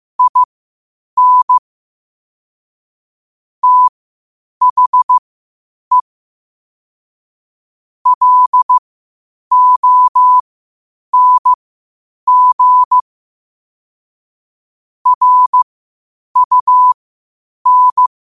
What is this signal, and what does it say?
Morse code, heard as a keyed tone, about 15 wpm, spaced as for 8 wpm, 1000 Hz. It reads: IN THE LONG RUN
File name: qotdfiles / 8wpm